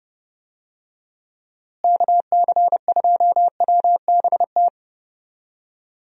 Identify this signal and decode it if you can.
Morse code, heard as a keyed tone, about 30 wpm, 700 Hz. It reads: KC2WBT
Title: USA Call Signs - 6 Characters - Repeated 3x - 30wpm